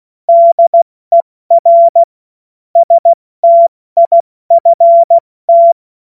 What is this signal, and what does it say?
Morse code, heard as a keyed tone, about 16 wpm, 680 Hz. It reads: DER STIFT
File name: Morsekurs / Witze / Witz 20. Januar 2026 70 F80